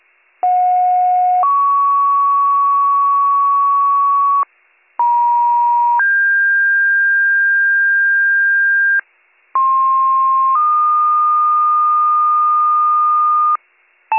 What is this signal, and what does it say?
Two-tone page, heard as a one-second tone, then a three-second tone, about 730 then 1120 Hz, about 950 then 1600 Hz, about 1060 then 1220 Hz. It is heard from 0.4 to 13.6 s.